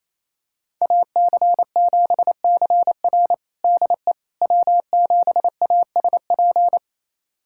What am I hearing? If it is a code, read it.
Morse code, heard as a keyed tone, about 28 wpm, 700 Hz. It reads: AC7CR DE W7ASP
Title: USA Callsign DE Callsign - 5 Characters - 28wpm